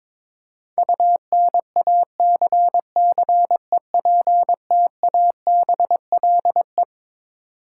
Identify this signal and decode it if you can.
Morse code, heard as a keyed tone, about 22 wpm, 700 Hz. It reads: UNACCEPTABLE
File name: Twelve Letter Words - 22wpm